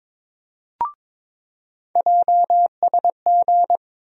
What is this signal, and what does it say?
Morse code, heard as a keyed tone, about 22 wpm, 700 Hz. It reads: JSG
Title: USA Call Sign Postfixes - 3 Letters 22wpm